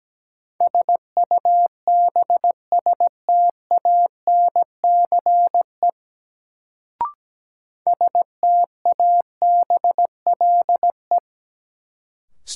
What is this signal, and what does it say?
Morse code, heard as a keyed tone, about 17 wpm, 700 Hz. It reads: SUBSTANCE STABLE
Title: Sentences from Top 1500 Words - New Word Drill Alt - 17wpm